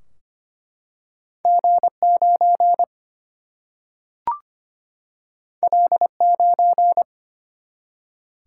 Morse code, heard as a keyed tone, about 25 wpm, 700 Hz. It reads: G9 L9
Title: Single-Letter Single-Number 25wpm